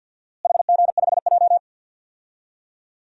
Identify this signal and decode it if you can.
Morse code, heard as a keyed tone, about 51 wpm, 700 Hz. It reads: HG5J
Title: DXCC-100 Call Signs - 4 Characters - Repeated 3x - 50wpm